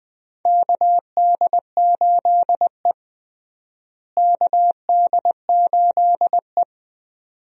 Morse code, heard as a keyed tone, about 20 wpm, 700 Hz. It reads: KD8E KD8E
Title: USA Call Signs - 4 Characters - Repeated 2x - 20wpm